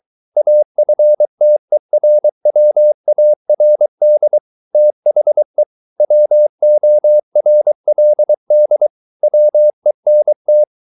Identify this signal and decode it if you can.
Morse code, heard as a keyed tone, about 23 wpm, 600 Hz. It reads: AFTERWARD THE WORLD WENT